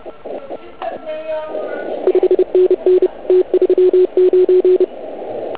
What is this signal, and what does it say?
Morse code, heard as a keyed tone, about 27 wpm, 380 Hz. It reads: 5NN T39